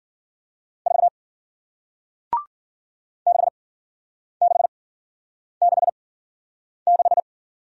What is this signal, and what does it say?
Morse code, heard as a keyed tone, about 53 wpm, spaced as for 9 wpm, 700 Hz. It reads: 4 6666